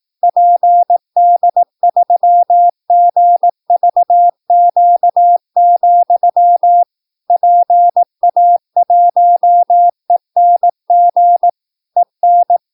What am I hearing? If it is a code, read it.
Morse code, heard as a keyed tone, about 18 wpm, 700 Hz. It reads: PD3GVQ, PA1ENG EN
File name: PI4AA Uitzending